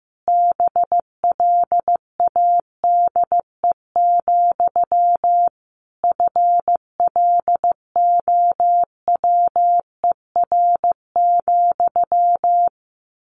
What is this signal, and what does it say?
Morse code, heard as a keyed tone, about 15 wpm, 700 Hz. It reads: BLADE, FLOWER,
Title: A Sound Of Thunder 15wpm